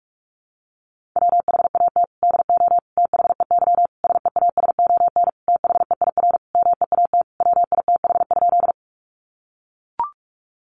Morse code, heard as a keyed tone, about 44 wpm, 700 Hz. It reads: WHAT DO THEY SEASON THEIR MEAT WITH?
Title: Sentences from Top 1400 Words 45wpm